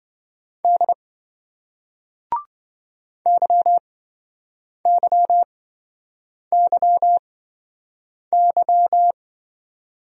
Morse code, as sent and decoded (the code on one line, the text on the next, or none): -.. / -.-- -.-- -.-- -.--
D YYYY